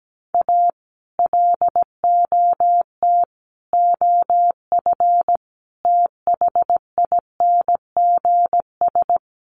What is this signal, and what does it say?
Morse code, heard as a keyed tone, about 17 wpm, 700 Hz. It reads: A LOT OF THINGS
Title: Sets of 5 Words 17wpm